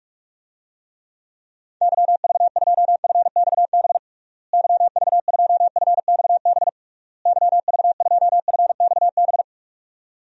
Morse code, heard as a keyed tone, about 47 wpm, 700 Hz. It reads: YV2FXB YV2FXB YV2FXB
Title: DXCC-100 Call Signs - 6 Characters - Repeated 3x - 45wpm